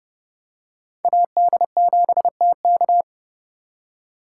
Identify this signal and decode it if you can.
Morse code, heard as a keyed tone, about 30 wpm, 700 Hz. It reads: AD7TK
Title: USA Call Signs - 4x Character Spacing - 30wpm